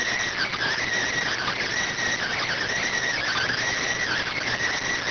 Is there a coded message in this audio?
Yes: Emergency Alert System (SAME) tones (header bursts) from 0.2 to 4.5 s, heard as harsh buzzing bursts.